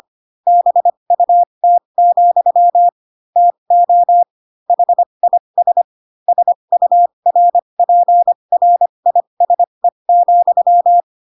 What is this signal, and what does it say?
Morse code, heard as a keyed tone, about 25 wpm, spaced as for 22 wpm, 700 Hz. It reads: BUT, TO HIS SURPRISE,